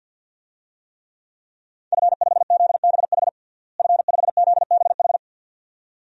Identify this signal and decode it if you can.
Morse code, heard as a keyed tone, about 52 wpm, 700 Hz. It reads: F5ZBH F5ZBH
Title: DXCC-100 Call Signs - 5 Characters - Repeated 2x - 50wpm